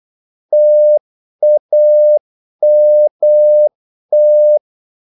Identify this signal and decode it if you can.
Morse code, heard as a keyed tone, about 8 wpm, 600 Hz. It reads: TAMT
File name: Korte ord DK 8wpm